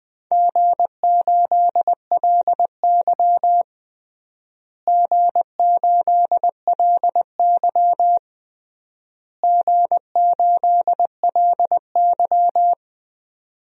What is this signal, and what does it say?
Morse code, heard as a keyed tone, about 19 wpm, 700 Hz. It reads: G8LY G8LY G8LY
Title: DXCC-100 Call Signs - 4 Characters - Repeated 3x - 20wpm